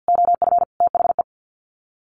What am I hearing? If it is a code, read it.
Morse code, heard as a keyed tone, about 49 wpm, 700 Hz. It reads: OF THE
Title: Sets of 5 Words Encore 50wpm